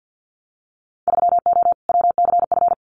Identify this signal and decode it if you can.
Morse code, heard as a keyed tone, about 50 wpm, 700 Hz. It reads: 3O WCF
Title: Winter Field Day Exchanges 50wpm